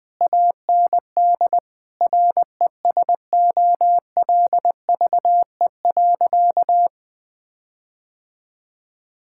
Morse code, heard as a keyed tone, about 20 wpm, 700 Hz. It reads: AND RESOLVE.
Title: Difficult Sentences with Hints - 20wpm